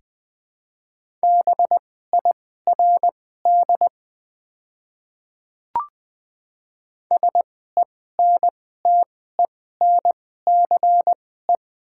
Morse code, heard as a keyed tone, about 20 wpm, spaced as for 15 wpm, 700 Hz. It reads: BIRD SENTENCE
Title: 2x Character Spacing - Top 500 Words - 20wpm